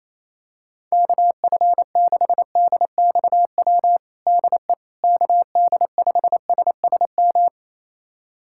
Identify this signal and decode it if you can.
Morse code, heard as a keyed tone, about 28 wpm, 700 Hz. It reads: KF6DXW DE KD5SSM
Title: USA Callsign DE Callsign - 6 Characters - 28wpm